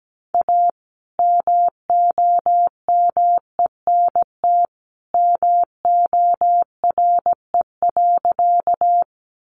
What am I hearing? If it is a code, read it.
Morse code, heard as a keyed tone, about 17 wpm, 700 Hz. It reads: A MOMENT MORE.
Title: Sentences from Top 1000 Words 17wpm